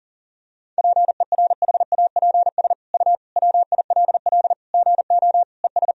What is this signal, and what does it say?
Morse code, heard as a keyed tone, about 40 wpm, 700 Hz. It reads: PERHAPS U WILL GO ES
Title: Sentences from Top 500 Words and Abbreviations 40wpm